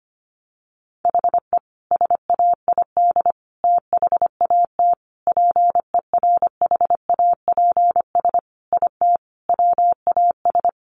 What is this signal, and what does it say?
Morse code, heard as a keyed tone, about 25 wpm, 700 Hz. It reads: HE SAID THAT PERHAPS IT WAS
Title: Sentences from Top 1200 Words 25wpm